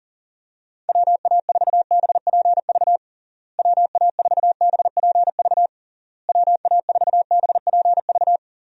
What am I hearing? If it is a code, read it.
Morse code, heard as a keyed tone, about 40 wpm, 700 Hz. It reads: WA4BPV WA4BPV WA4BPV
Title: USA Call Signs - 6 Characters - Repeated 3x - 40wpm